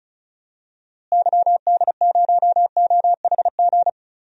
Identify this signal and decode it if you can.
Morse code, heard as a keyed tone, about 36 wpm, 700 Hz. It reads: YD0OHG
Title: DXCC-100 Call Signs - 6 Characters - Repeated 2x - 35wpm